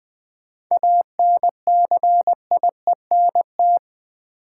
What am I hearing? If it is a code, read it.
Morse code, heard as a keyed tone, about 21 wpm, 700 Hz. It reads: ANCIENT
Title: 7 Letter Words - First 3 Letters Given - 20wpm